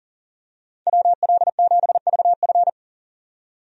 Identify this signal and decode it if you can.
Morse code, heard as a keyed tone, about 40 wpm, 700 Hz. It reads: WL7VF
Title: USA Call Signs - 5 Characters - 40wpm